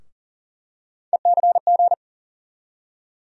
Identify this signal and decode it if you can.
Morse code, heard as a keyed tone, about 40 wpm, 700 Hz. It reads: ECG
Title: USA Call Sign Postfixes - 3 Letters 40wpm